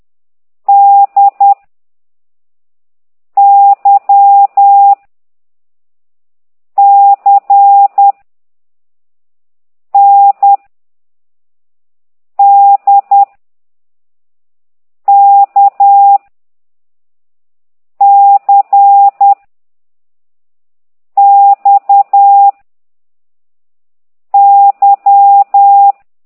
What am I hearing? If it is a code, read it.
Morse code, heard as a keyed tone, about 10 wpm, 800 Hz. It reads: D Y C N D K C X Y
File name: ct2idw / exerc4